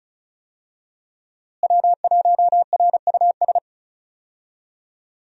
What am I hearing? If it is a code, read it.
Morse code, heard as a keyed tone, about 36 wpm, 700 Hz. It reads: W1RUS
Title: USA Call Signs - 5 Characters - Repeated 2x - 35wpm